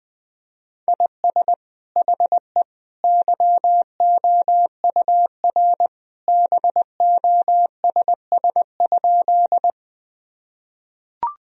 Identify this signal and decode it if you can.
Morse code, heard as a keyed tone, about 21 wpm, 700 Hz. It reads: IS HE YOUR BOSS?
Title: Sentences from Top 1000 Words - Review Part 4 - 20wpm